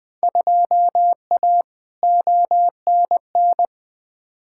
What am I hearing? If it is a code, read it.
Morse code, heard as a keyed tone, about 20 wpm, 700 Hz. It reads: 2A ONN
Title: Field Day Exchanges Encore - 20wpm